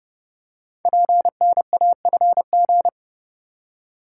Morse code, heard as a keyed tone, about 28 wpm, 700 Hz. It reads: PNAFG
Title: Code Groups - 5 Letters - 30wpm